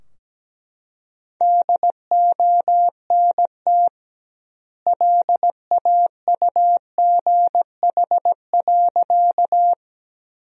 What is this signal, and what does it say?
Morse code, heard as a keyed tone, about 17 wpm, 700 Hz. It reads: DONT LAUGH.
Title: 2x Word-Spacing - Sentences from Top 500 Words - 17wpm